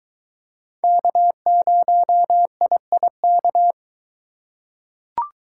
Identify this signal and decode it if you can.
Morse code, heard as a keyed tone, about 22 wpm, 700 Hz. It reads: K0IIK